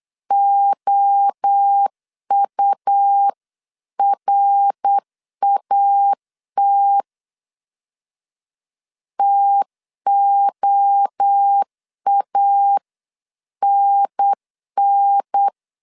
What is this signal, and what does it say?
Morse code, heard as a keyed tone, about 8 wpm, 800 Hz. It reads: OURAT TOANN
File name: CW / DET / 1 3 1500C S C 8wpm 4214